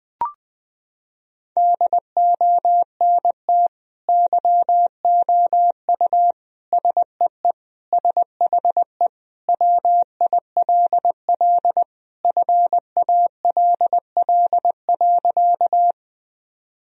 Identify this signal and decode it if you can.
Morse code, heard as a keyed tone, about 20 wpm, 700 Hz. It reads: DONT YOU SEE SHE WILL FALL.
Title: Sentences from Top 1000 Words - Review Part 2 - 20wpm